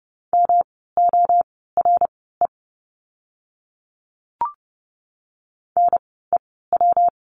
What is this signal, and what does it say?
Morse code, heard as a keyed tone, about 30 wpm, spaced as for 17 wpm, 700 Hz. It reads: MORE NEW
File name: course / 30:10wpm / 237 Lesson 38f - Letter q - Words 30wpm